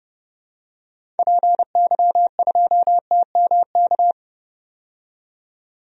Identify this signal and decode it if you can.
Morse code, heard as a keyed tone, about 30 wpm, 700 Hz. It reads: PY2TMK
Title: DXCC-100 Call Signs - 6 Characters - 30wpm